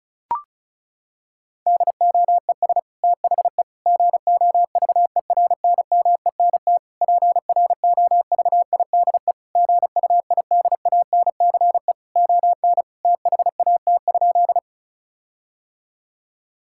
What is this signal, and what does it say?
Morse code, heard as a keyed tone, about 34 wpm, 700 Hz. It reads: DOES THE GOVERNMENT PROVIDE GUIDANCE ON THAT?
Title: Sentences from Top 1600 Words 35wpm